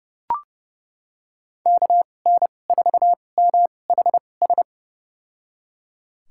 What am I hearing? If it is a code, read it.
Morse code, heard as a keyed tone, about 29 wpm, spaced as for 21 wpm, 700 Hz. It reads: KN4MHS
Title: USA Call Signs - 6 Characters - 2x Character Spacing - 30wpm